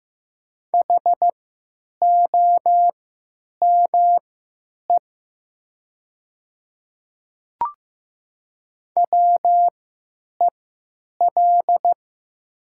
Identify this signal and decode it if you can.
Morse code, heard as a keyed tone, about 15 wpm, spaced as for 9 wpm, 700 Hz. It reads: HOME WEL
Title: 3x Character Spacing - Top 300 Words - 15wpm